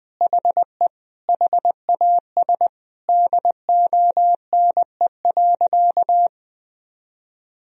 Morse code, heard as a keyed tone, about 20 wpm, 700 Hz. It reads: HE HAS DONE.
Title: Sentences from Top 1000 Words - Review Part 1 - 20wpm